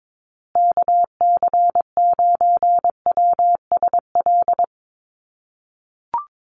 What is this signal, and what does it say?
Morse code, heard as a keyed tone, about 22 wpm, 700 Hz. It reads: KC9WSL
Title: USA Call Signs Repeated 2x 22wpm